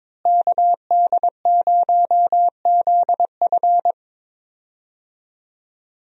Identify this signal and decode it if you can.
Morse code, heard as a keyed tone, about 22 wpm, 700 Hz. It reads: KD0ZF
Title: USA Call Signs - 5 Characters - Repeated 3x - 22wpm